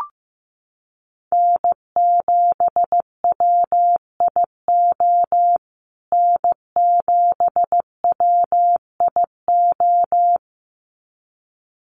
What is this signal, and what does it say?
Morse code, heard as a keyed tone, about 15 wpm, 700 Hz. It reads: N7WIO N7WIO
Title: USA Call Signs Repeated 2x 15wpm